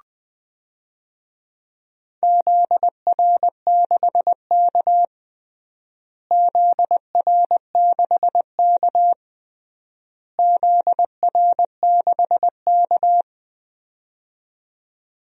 Morse code, heard as a keyed tone, about 20 wpm, 700 Hz. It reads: ZR6K ZR6K ZR6K
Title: DXCC-100 Call Signs - 4 Characters - Repeated 3x - 20wpm